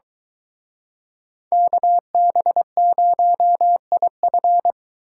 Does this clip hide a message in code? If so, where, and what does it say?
Yes: Morse code from 1.5 to 4.7 s, heard as a keyed tone, about 23 wpm, 700 Hz. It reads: KB0IF